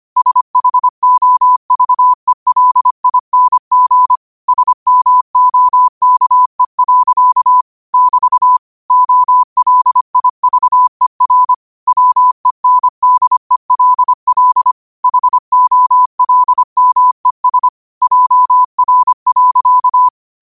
Morse code, heard as a keyed tone, about 25 wpm, 1000 Hz. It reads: SHOVELING SMOKE. <BT> OLIVER WENDELL HOLMES JR.
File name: qotdfiles / 25wpm